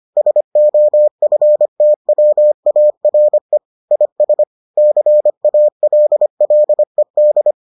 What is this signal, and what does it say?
Morse code, heard as a keyed tone, about 25 wpm, 600 Hz. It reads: SOFTWARE IS CALLED